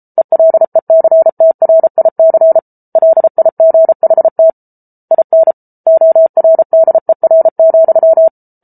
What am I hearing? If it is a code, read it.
Morse code, heard as a keyed tone, about 34 wpm, 650 Hz. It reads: ELECTRIC LIGHT IN ORDER,
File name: Book Two / 07-Chapter 7 - 33 WPM